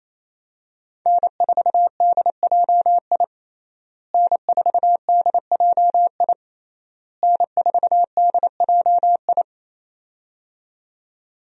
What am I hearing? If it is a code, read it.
Morse code, heard as a keyed tone, about 28 wpm, 700 Hz. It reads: N4DJI N4DJI N4DJI
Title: USA Call Signs - 5 Characters - Repeated 3x - 28wpm